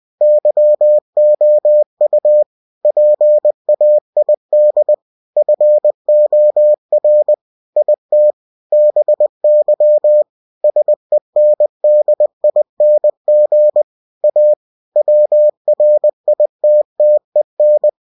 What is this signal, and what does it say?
Morse code, heard as a keyed tone, about 20 wpm, 600 Hz. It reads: YOU PAID FOR IT BY SENDING A WRITTEN
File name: cw-books / winnie-the-pooh / winnie-20wpm-page-0119